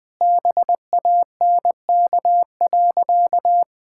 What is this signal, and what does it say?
Morse code, heard as a keyed tone, about 20 wpm, 700 Hz. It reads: BANK.
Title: Sentences from Top 1000 Words - Review Part 6 - 20wpm